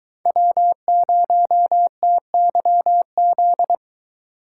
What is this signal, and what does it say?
Morse code, heard as a keyed tone, about 22 wpm, 700 Hz. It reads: W0TYZ